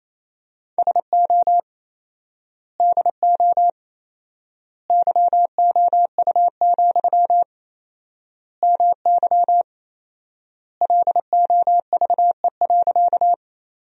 Morse code, heard as a keyed tone, about 28 wpm, 700 Hz. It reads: SO DO YOU, MY LOVE.
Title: 4x Word-Spacing - Sentences from Top 400 Words - 28wpm